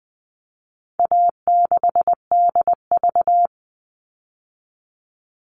Morse code, heard as a keyed tone, about 20 wpm, 700 Hz. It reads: A6DV
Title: All Cut Numbers - 4-Digit 20wpm